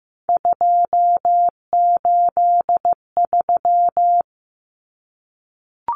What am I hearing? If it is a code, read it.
Morse code, heard as a keyed tone, about 15 wpm, 700 Hz. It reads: 283
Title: Common Cut Numbers - 3-Digit 15wpm